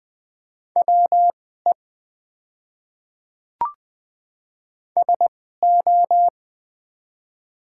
Morse code, heard as a keyed tone, about 20 wpm, spaced as for 15 wpm, 700 Hz. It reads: WE SO
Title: 2x Character Spacing - Top 200 Words - 20wpm